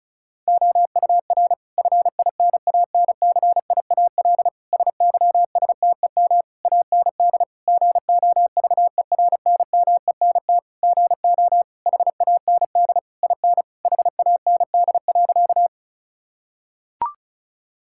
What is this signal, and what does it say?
Morse code, heard as a keyed tone, about 34 wpm, 690 Hz. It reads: OUR FINANCIAL SYSTEM AND GOVERNMENT GO HAND IN HAND.
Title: Sentences from Top 1000 Words - Review Part 5 - 35wpm